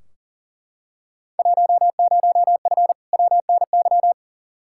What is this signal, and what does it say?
Morse code, heard as a keyed tone, about 40 wpm, 700 Hz. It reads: 10F WNY
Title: Field Day Exchanges Encore - 40wpm